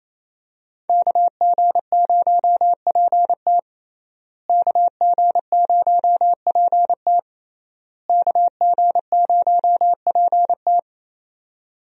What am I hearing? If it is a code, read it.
Morse code, heard as a keyed tone, about 28 wpm, 700 Hz. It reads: KG0PT KG0PT KG0PT
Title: USA Call Signs - 5 Characters - Repeated 3x - 28wpm